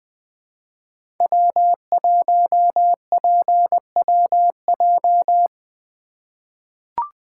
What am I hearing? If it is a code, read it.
Morse code, heard as a keyed tone, about 21 wpm, 700 Hz. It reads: W1PWJ
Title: USA Call Signs - 5 Characters - 4x Character Spacing - 20wpm